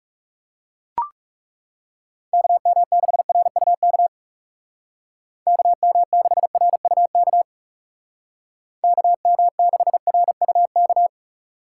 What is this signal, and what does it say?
Morse code, heard as a keyed tone, about 37 wpm, 700 Hz. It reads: KM6RUK KM6RUK KM6RUK
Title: Speed Racing - USA Call Signs 30wpm